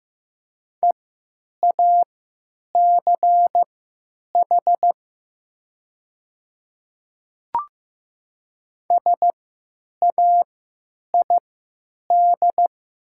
Morse code, heard as a keyed tone, about 15 wpm, spaced as for 9 wpm, 700 Hz. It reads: EACH SAID
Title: 3x Character Spacing - Top 200 Words - 15wpm